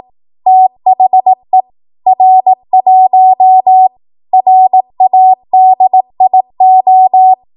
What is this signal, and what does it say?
Morse code, heard as a keyed tone, about 18 wpm, 750 Hz. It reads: THE R1 RADIO
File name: Archive / 18 WPM / 18WPM